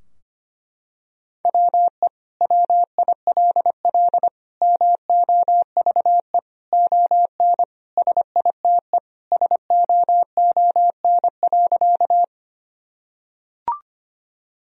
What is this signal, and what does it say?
Morse code, heard as a keyed tone, about 25 wpm, 700 Hz. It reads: WE WILL MOVE ON SITE SOON.
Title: Sentences from Top 1000 Words - Review Part 4 - 25wpm